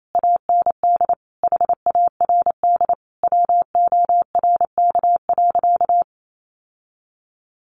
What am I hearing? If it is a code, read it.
Morse code, heard as a keyed tone, about 28 wpm, 700 Hz. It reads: AND HARD WORK.
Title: Sentences from Top 1300 Words 28wpm